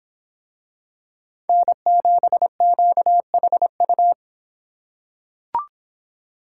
Morse code, heard as a keyed tone, about 25 wpm, 700 Hz. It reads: N7QHU